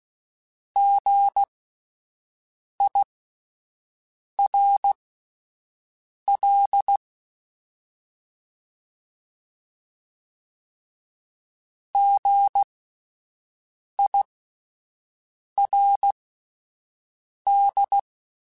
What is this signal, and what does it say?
Morse code, heard as a keyed tone, about 16 wpm, spaced as for 5 wpm, 800 Hz. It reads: GIRL GIRD
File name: Spaced Alphas / Spaced Alphas mp3s / Gs sp @10